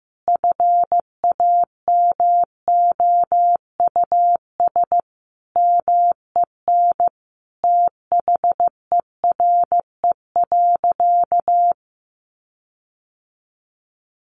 Morse code, heard as a keyed tone, about 15 wpm, 700 Hz. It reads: FAMOUS MEN THERE.
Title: Sentences from Top 1500 Words 15wpm